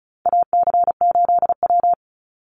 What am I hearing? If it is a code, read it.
Morse code, heard as a keyed tone, about 34 wpm, 700 Hz. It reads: AC8W
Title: Call Signs Repeated 3x - 35wpm